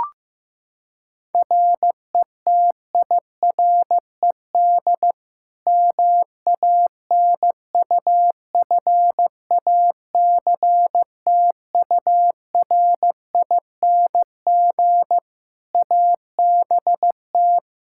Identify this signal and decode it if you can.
Morse code, heard as a keyed tone, about 15 wpm, 700 Hz. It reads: RETIRED MANUFACTURING ABT
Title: Practice QSO - Exchange 3 - More Info - 15wpm